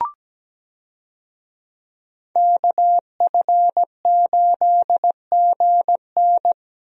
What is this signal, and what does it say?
Morse code, heard as a keyed tone, about 17 wpm, 700 Hz. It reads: KF8GN
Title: USA Call Signs - 5 Characters - Repeated 2x - 17wpm